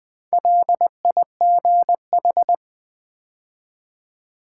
Morse code, heard as a keyed tone, about 20 wpm, 700 Hz. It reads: LIGH
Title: Build A Word - 2 Letters At A Time - 8-Letter Words - 20wpm